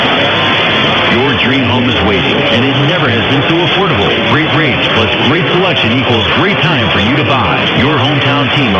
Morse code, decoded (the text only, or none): none